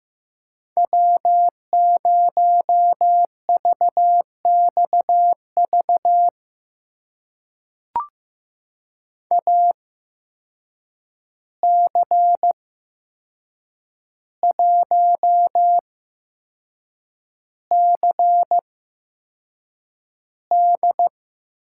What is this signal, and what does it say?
Morse code, heard as a keyed tone, about 15 wpm, spaced as for 6 wpm, 700 Hz. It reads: W0VXV AC1CD
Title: USA Call Signs - 5 Characters - 8x Character Spacing - 15wpm